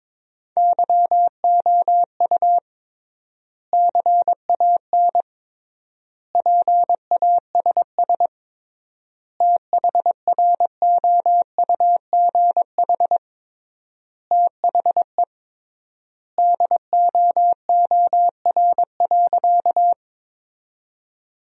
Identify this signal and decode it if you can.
Morse code, heard as a keyed tone, about 22 wpm, 700 Hz. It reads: YOU CAN PASS THROUGH THE DOOR.
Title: 3x Word-Spacing - Sentences from Top 400 Words - 22wpm